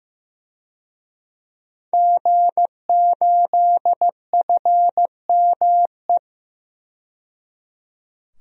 Morse code, heard as a keyed tone, about 15 wpm, 700 Hz. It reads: G8FME
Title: DXCC-100 Call Signs - 5 Characters - 15wpm